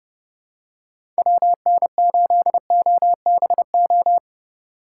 Morse code, heard as a keyed tone, about 30 wpm, 700 Hz. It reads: WN8OBO
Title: USA Call Signs - 4x Character Spacing - 30wpm